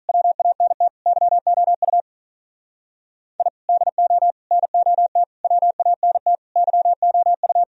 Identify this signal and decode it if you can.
Morse code, heard as a keyed tone, about 42 wpm, 700 Hz. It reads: WANT YOU I DO NOT WANT YOU
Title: Speed-Racing - Sets of 5 Words 35wpm